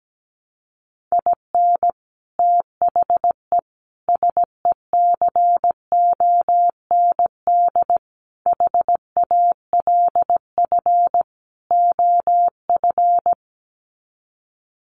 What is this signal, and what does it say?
Morse code, heard as a keyed tone, about 17 wpm, 700 Hz. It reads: IN THE SECOND HALF OF
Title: Sets of 5 Words 17wpm